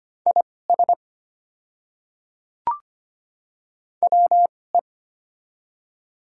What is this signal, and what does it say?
Morse code, heard as a keyed tone, about 25 wpm, spaced as for 18 wpm, 700 Hz. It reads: IS WE
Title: 2x Character Spacing - Top 100 Words - 25wpm